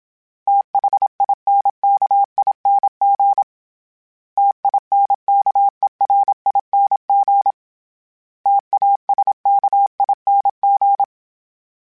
Morse code, heard as a keyed tone, about 27 wpm, 800 Hz. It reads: THINKING TINKERING TASKING
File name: Special INGs-mp3s / Special INGs-mp3s / Sngl Ing @25